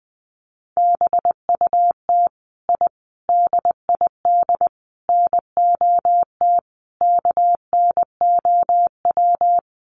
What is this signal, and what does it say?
Morse code, heard as a keyed tone, about 19 wpm, 700 Hz. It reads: BUT I DID NOT KNOW
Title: Sets of 5 Words Encore 20wpm